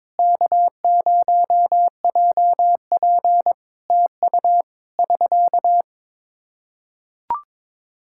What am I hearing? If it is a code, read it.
Morse code, heard as a keyed tone, about 22 wpm, 700 Hz. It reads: K0JP TU <SK>
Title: Practice QSO - Exchange 4 - Ending - 22wpm